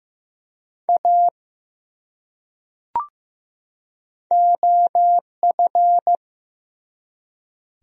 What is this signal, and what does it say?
Morse code, heard as a keyed tone, about 15 wpm, 700 Hz. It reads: A OF
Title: Top 100 Words 15wpm